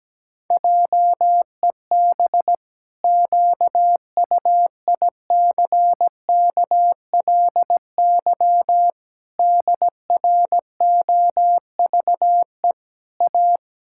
Morse code, heard as a keyed tone, about 17 wpm, 700 Hz. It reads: JEB QUICKLY DROVE A
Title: Pangram Challenge v2 - 17wpm